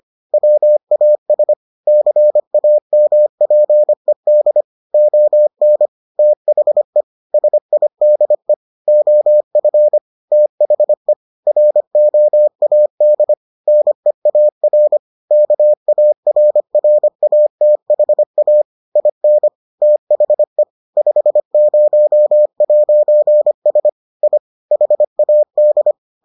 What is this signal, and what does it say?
Morse code, heard as a keyed tone, about 25 wpm, 600 Hz. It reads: WAS CAMPED ON THE SIDE OF THE ROAD NEAR KARRATHA IN THE 50'S I HAD